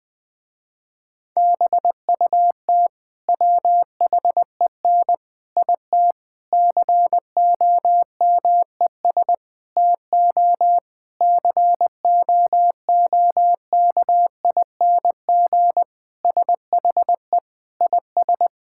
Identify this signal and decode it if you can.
Morse code, heard as a keyed tone, about 20 wpm, 700 Hz. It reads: BUT WHEN IT COMES TO COOKING SHE IS
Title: Sentences with 5-Word Phrases - Spoken Beginning - 20wpm